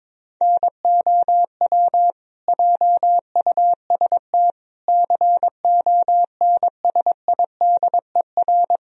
Morse code, heard as a keyed tone, about 22 wpm, 700 Hz. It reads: NOW JUST CONSIDER
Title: Sentences from Top 1000 Words - Review Part 4 - 22wpm